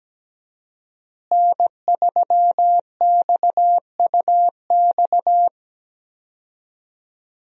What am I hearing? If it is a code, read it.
Morse code, heard as a keyed tone, about 17 wpm, 700 Hz. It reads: N3XUX
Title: USA Call Signs - 5 Characters - Repeated 2x - 17wpm